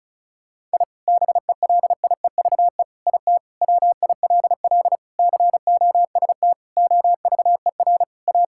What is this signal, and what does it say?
Morse code, heard as a keyed tone, about 36 wpm, 700 Hz. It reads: I BELIEVE IT WILL COST OVER A
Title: Sentences from Top 1000 Words - Review Part 5 - 35wpm